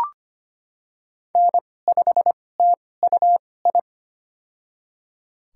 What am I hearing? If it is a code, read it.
Morse code, heard as a keyed tone, about 25 wpm, spaced as for 18 wpm, 700 Hz. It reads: N5TUI
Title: USA Call Signs - 2x Character Spacing - 25wpm